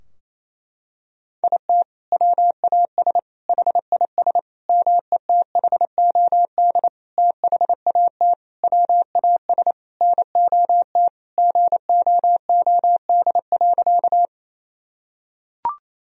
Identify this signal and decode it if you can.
Morse code, heard as a keyed tone, about 28 wpm, 700 Hz. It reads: IT WAS HIS METHOD THAT WAS NOT GOOD.
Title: Sentences from Top 1000 Words - Review Part 3 - 28wpm